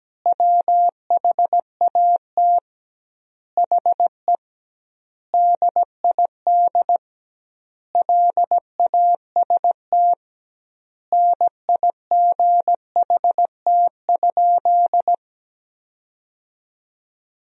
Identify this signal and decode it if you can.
Morse code, heard as a keyed tone, about 17 wpm, 700 Hz. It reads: WHAT HE DID LAST NIGHT?
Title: 2x Word-Spacing - Sentences from Top 300 Words - 17wpm